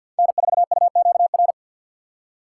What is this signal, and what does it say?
Morse code, heard as a keyed tone, about 47 wpm, 700 Hz. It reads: N4UQR
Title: USA Call Signs - 5 Characters - 3x Character Spacing - 50wpm